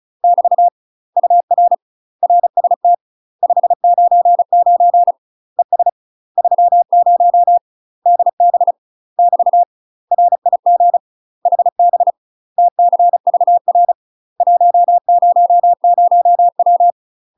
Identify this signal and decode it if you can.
Morse code, heard as a keyed tone, about 34 wpm, 700 Hz. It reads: <BT> UR RST 599 ES 30 DB <BT> RIG HB TCVR 100W
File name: UK MASTER QSOs TRAINING / 35WPM / UK MASTER QSOs 0000